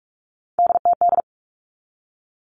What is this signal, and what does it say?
Morse code, heard as a keyed tone, about 44 wpm, 700 Hz. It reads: DTD
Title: All Cut Numbers - 3-Digit 45wpm